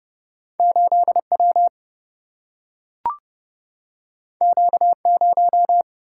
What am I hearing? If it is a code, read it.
Morse code, heard as a keyed tone, about 30 wpm, 700 Hz. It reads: 8W Q0
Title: Single-Letter Single-Number 30wpm